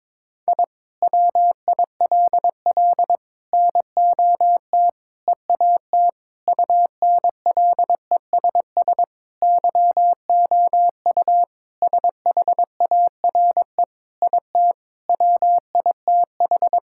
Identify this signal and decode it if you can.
Morse code, heard as a keyed tone, about 22 wpm, 700 Hz. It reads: I WILL NOT EAT UNLESS YOU SHARE IT WITH
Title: Sentences from Top 1000 Words - Review Part 6 - 22wpm